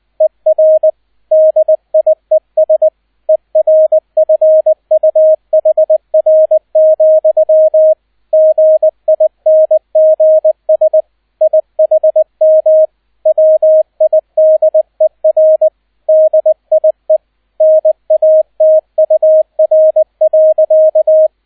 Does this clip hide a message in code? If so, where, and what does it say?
Yes: Morse code from 0.2 to 21.3 s, heard as a keyed tone, about 20 wpm, 630 Hz, with radio noise. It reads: ER DIES ERFUHR, GINGS IHM WIDER DIE NATUR.